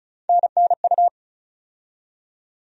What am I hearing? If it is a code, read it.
Morse code, heard as a keyed tone, about 36 wpm, 700 Hz. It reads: NNU
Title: USA Call Sign Postfixes - 3 Letters 35wpm